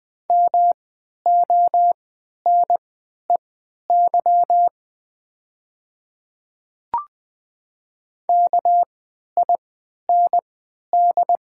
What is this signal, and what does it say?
Morse code, heard as a keyed tone, about 20 wpm, spaced as for 11 wpm, 700 Hz. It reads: MONEY KIND
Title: 3x Character Spacing - Top 500 Words - 20wpm